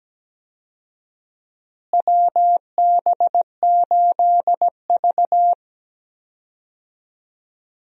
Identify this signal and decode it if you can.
Morse code, heard as a keyed tone, about 17 wpm, 700 Hz. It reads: WB8V
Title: USA Call Signs - 4 Characters - Repeated 3x - 17wpm